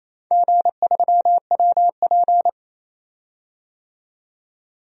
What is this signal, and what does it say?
Morse code, heard as a keyed tone, about 28 wpm, 700 Hz. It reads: G3WP
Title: DXCC-100 Call Signs - 4 Characters - 28wpm